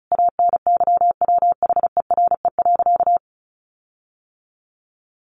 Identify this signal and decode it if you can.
Morse code, heard as a keyed tone, about 34 wpm, 700 Hz. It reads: ANYWHERE.
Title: Sentences from Top 1400 Words 35wpm